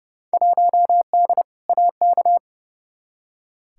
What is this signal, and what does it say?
Morse code, heard as a keyed tone, about 31 wpm, 700 Hz. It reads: 1D AK
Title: Field Day Exchanges - 30wpm